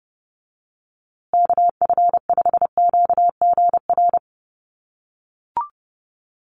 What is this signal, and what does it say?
Morse code, heard as a keyed tone, about 30 wpm, 700 Hz. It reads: KF5QGR
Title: USA Call Signs Repeated 3x 30wpm